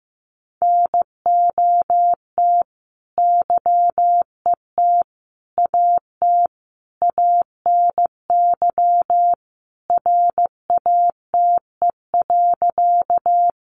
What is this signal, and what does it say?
Morse code, heard as a keyed tone, about 15 wpm, 700 Hz. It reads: NOT YET AT ANY RATE.
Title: Sentences from Top 700 Words 15wpm